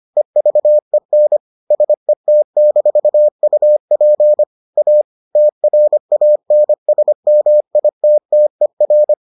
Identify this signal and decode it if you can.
Morse code, heard as a keyed tone, about 25 wpm, 600 Hz. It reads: EVEN SET-UP A TRANSMITTER